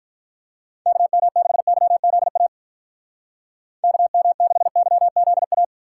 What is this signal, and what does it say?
Morse code, heard as a keyed tone, about 43 wpm, 700 Hz. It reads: KM6YZA KM6YZA
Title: Speed Racing - USA Call Signs 35wpm